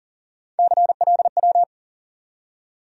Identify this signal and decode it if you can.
Morse code, heard as a keyed tone, about 40 wpm, 700 Hz. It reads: CLW
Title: USA Call Sign Postfixes - 3 Letters 40wpm